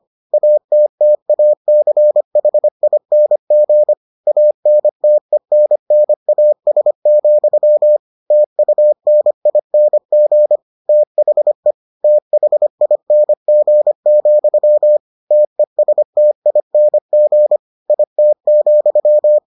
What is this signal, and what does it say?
Morse code, heard as a keyed tone, about 25 wpm, 600 Hz. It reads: ATTACHING ANTENNAS, TUNING THE THING, TESTING IT,